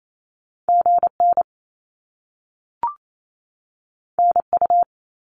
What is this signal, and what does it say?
Morse code, heard as a keyed tone, about 28 wpm, 700 Hz. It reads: GN NU
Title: Top Two Letter Combinations in Words 28wpm